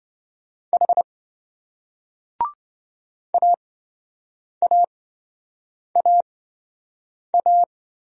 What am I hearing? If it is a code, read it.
Morse code, heard as a keyed tone, about 28 wpm, spaced as for 7 wpm, 700 Hz. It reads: H AAAA